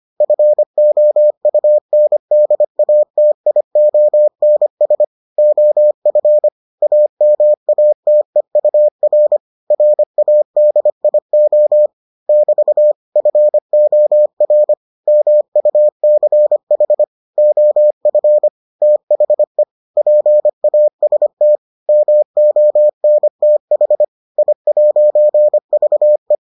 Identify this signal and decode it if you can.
Morse code, heard as a keyed tone, about 25 wpm, 600 Hz. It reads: FOUNDATIONS OF AMATEUR RADIO <BT> FOR MUCH OF THE PAST MONTH I'VE